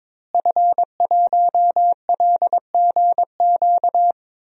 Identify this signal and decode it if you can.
Morse code, heard as a keyed tone, about 22 wpm, 700 Hz. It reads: F1LGQ
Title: DXCC-100 Call Signs - 5 Characters - Repeated 2x - 22wpm